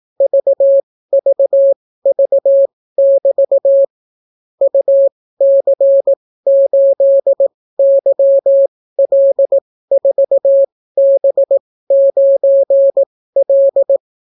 Morse code, heard as a keyed tone, about 18 wpm, spaced as for 14 wpm, 550 Hz. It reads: VVV<BT> UC8YL4B9L